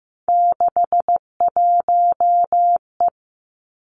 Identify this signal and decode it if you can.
Morse code, heard as a keyed tone, about 15 wpm, 700 Hz. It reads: 61E
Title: Three Challenging Letter-Number Combinations 15wpm